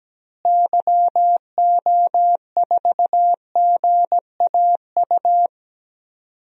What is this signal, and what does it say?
Morse code, heard as a keyed tone, about 17 wpm, 700 Hz. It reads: YO4GAU
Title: DXCC-100 Call Signs - 6 Characters - Repeated 3x - 17wpm